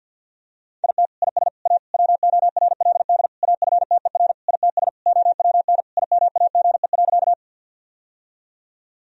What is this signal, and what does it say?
Morse code, heard as a keyed tone, about 50 wpm, 700 Hz. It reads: IT IS A WORLD AFTER ITS OWN IMAGE.
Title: Sentences from Top 1000 Words - Review Part 4 - 50wpm